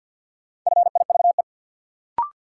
Morse code, heard as a keyed tone, about 51 wpm, 700 Hz. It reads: FIVE
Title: Sentences from Top 400 Words - New Word Drill - 50wpm